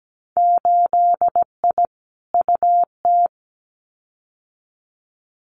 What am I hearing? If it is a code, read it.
Morse code, heard as a keyed tone, about 17 wpm, 700 Hz. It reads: 8I UT
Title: Winter Field Day Exchanges Encore 17wpm